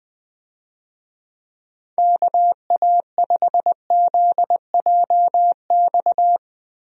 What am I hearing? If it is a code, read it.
Morse code, heard as a keyed tone, about 19 wpm, 700 Hz. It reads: KA5ZJX
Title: USA Call Signs - 6 Characters - Repeated 3x - 20wpm